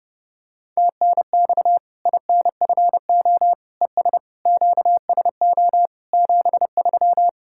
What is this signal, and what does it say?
Morse code, heard as a keyed tone, about 31 wpm, 700 Hz. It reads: TNX INFO ES QSO 73
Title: Practice QSO - Exchange 4 - Ending - 30wpm